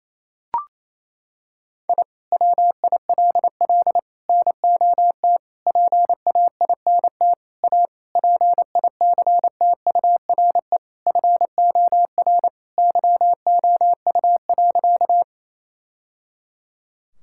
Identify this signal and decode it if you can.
Morse code, heard as a keyed tone, about 28 wpm, 700 Hz. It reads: I WILL NOT PAINT A PICTURE FOR YOU.
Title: Sentences from Top 1000 Words - Review Part 6 - 28wpm